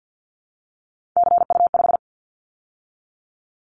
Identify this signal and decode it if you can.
Morse code, heard as a keyed tone, about 50 wpm, 700 Hz. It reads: CU5
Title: Callsign Prefixes for Top 100 DXCC - 50wpm